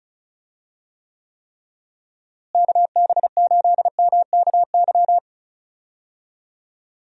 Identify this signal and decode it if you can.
Morse code, heard as a keyed tone, about 34 wpm, 700 Hz. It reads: KB8MKY
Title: USA Call Signs - 6 Characters - Repeated 3x - 35wpm